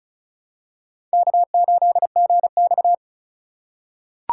Morse code, heard as a keyed tone, about 36 wpm, 700 Hz. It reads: K8GX
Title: USA Call Signs - 4 Characters - 4x Character Spacing - 35wpm